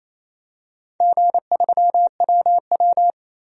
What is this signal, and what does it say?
Morse code, heard as a keyed tone, about 28 wpm, 700 Hz. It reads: G3WW
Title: DXCC-100 Call Signs - 4 Characters - Repeated 3x - 28wpm